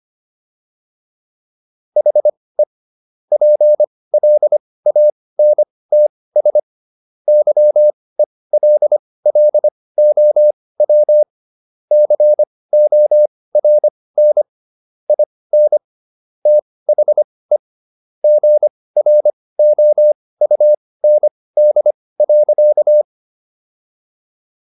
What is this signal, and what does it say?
Morse code, heard as a keyed tone, about 25 wpm, spaced as for 18 wpm, 600 Hz. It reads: HE PLANTS YELLOW CORN IN THE GROUND.